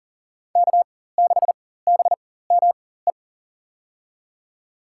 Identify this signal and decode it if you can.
Morse code, heard as a keyed tone, about 40 wpm, spaced as for 19 wpm, 700 Hz. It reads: K6BME
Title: USA Call Signs - 5 Characters - 4x Character Spacing - 40wpm